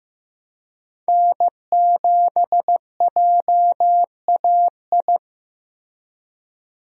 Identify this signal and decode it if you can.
Morse code, heard as a keyed tone, about 15 wpm, 700 Hz. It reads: N7JAI
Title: USA Call Signs - 5 Characters - Repeated 3x - 15wpm